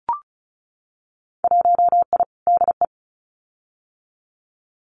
Morse code, heard as a keyed tone, about 34 wpm, 700 Hz. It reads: 1I DE
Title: Winter Field Day Exchanges Encore 35wpm